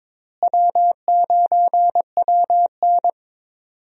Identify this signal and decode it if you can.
Morse code, heard as a keyed tone, about 21 wpm, 700 Hz. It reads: W9WN
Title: USA Call Signs - 4 Characters - Repeated 3x - 22wpm